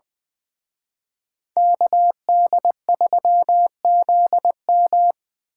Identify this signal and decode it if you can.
Morse code, heard as a keyed tone, about 24 wpm, 700 Hz. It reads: KD3ZM
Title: Speed Racing - USA 2x2 Call Signs 20wpm